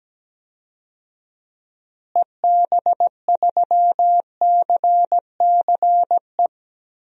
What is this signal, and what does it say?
Morse code, heard as a keyed tone, about 17 wpm, 700 Hz. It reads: EB3CCE
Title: DXCC-100 Call Signs - 6 Characters - Repeated 3x - 17wpm